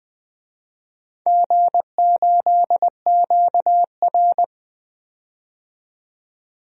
Morse code, heard as a keyed tone, about 20 wpm, 700 Hz. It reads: G8QR
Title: DXCC-100 Call Signs - 4 Characters - 20wpm